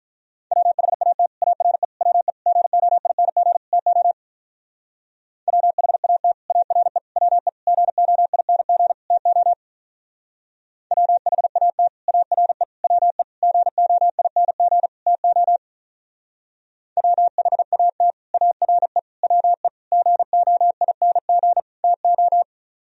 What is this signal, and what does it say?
Morse code, heard as a keyed tone, about 44 wpm, 700 Hz. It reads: WHAT ARE WE GOING TO WHAT ARE WE GOING TO WHAT ARE WE GOING TO WHAT ARE WE GOING TO
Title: Speed-Racing - Sets of 5 Words 35wpm